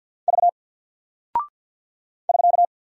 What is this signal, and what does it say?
Morse code, heard as a keyed tone, about 48 wpm, 700 Hz. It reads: V <SK>
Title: Single Character (Letter:Number:Punctuation:Prosign) Alt 50wpm